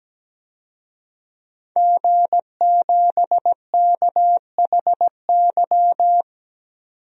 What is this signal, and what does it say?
Morse code, heard as a keyed tone, about 17 wpm, 700 Hz. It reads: G7KHY
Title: DXCC-100 Call Signs - 5 Characters - 17wpm